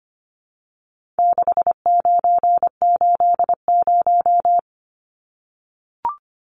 Morse code, heard as a keyed tone, about 25 wpm, 700 Hz. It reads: 6980
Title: Numbers - 4-Digit 25wpm